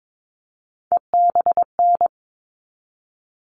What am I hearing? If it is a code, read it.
Morse code, heard as a keyed tone, about 22 wpm, 700 Hz. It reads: EBN
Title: All Cut Numbers - 3-Digit 22wpm